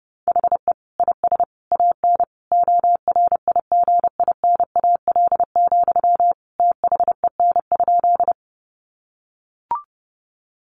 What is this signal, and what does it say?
Morse code, heard as a keyed tone, about 30 wpm, 700 Hz. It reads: HE IS AN ORIGINAL, THEN?
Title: Sentences from Top 900 Words 30wpm